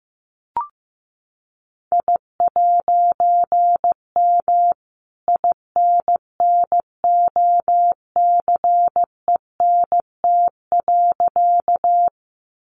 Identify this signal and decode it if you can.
Morse code, heard as a keyed tone, about 15 wpm, 700 Hz. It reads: I'M INNOCENT.
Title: A Sound Of Thunder 15wpm